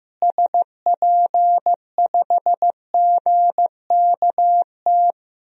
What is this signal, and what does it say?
Morse code, heard as a keyed tone, about 15 wpm, 700 Hz. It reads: SP5GKT
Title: DXCC-100 Call Signs - 6 Characters - Repeated 3x - 15wpm